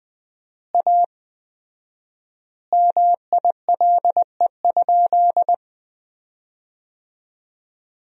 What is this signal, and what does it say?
Morse code, heard as a keyed tone, about 21 wpm, 700 Hz. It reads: A MILE?
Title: 4x Word-Spacing - Sentences from Top 400 Words - 20wpm